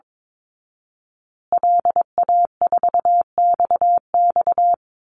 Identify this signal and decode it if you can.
Morse code, heard as a keyed tone, about 22 wpm, 700 Hz. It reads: LA4XX
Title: Call Signs Repeated 2x - 22wpm